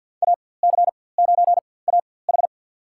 Morse code, heard as a keyed tone, about 52 wpm, spaced as for 23 wpm, 700 Hz. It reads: AC9AH
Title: USA Call Signs - 5 Characters - 4x Character Spacing - 50wpm